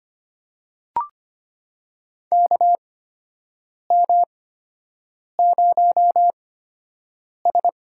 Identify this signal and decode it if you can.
Morse code, heard as a keyed tone, about 25 wpm, spaced as for 7 wpm, 700 Hz. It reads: KM0S
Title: USA Call Signs - 6 Characters - 8x Character Spacing - 25wpm